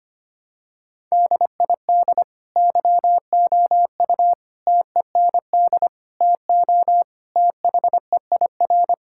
Morse code, heard as a keyed tone, about 25 wpm, 700 Hz. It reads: DID YOU TEND TO THEIR
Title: Sentences from Top 1000 Words - Review Part 5 - 25wpm